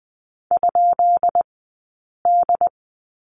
Morse code, heard as a keyed tone, about 20 wpm, spaced as for 8 wpm, 700 Hz. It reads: ?D
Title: 183 Lesson 30d - Letter b - Two Characters 20wpm